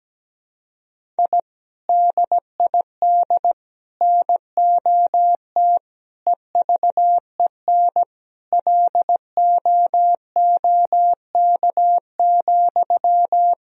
Morse code, heard as a keyed tone, about 17 wpm, 700 Hz. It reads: I DID NOT EVEN LOOK,
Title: Sentences from Top 1000 Words - Review Part 1 - 17wpm